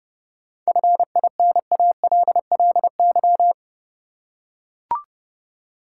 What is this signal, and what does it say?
Morse code, heard as a keyed tone, about 30 wpm, 700 Hz. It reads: FINALLY
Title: Seven Letter Words - 30wpm